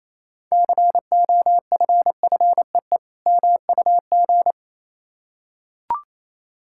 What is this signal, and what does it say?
Morse code, heard as a keyed tone, about 27 wpm, 700 Hz. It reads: COFFEE MUG
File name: Easy Words - Speed-Racing with Hints - 22wpm